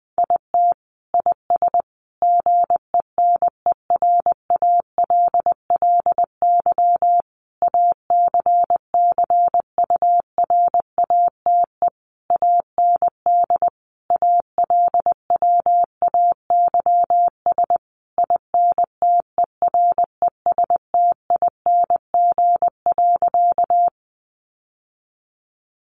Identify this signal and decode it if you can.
Morse code, heard as a keyed tone, about 20 wpm, 700 Hz. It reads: IT IS GENERALLY ACCURATE AND ALWAYS INTERESTING.
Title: Sentences from Top 1200 Words 20wpm